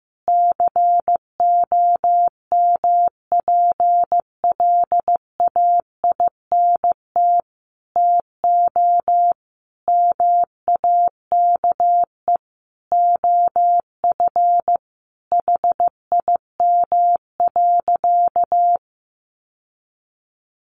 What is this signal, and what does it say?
Morse code, heard as a keyed tone, about 15 wpm, 700 Hz. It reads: COMPLAINT TO MAKE OF HIM.
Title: Sentences from Top 1400 Words 15wpm